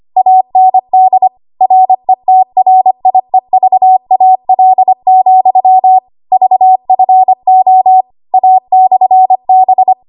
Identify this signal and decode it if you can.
Morse code, heard as a keyed tone, about 25 wpm, 750 Hz. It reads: AND RETRIEVAL, VFO A/B
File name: Archive / 25 WPM / 25WPM